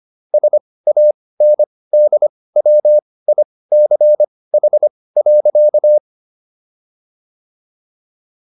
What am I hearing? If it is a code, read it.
Morse code, heard as a keyed tone, about 25 wpm, spaced as for 18 wpm, 600 Hz. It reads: SANDWICH.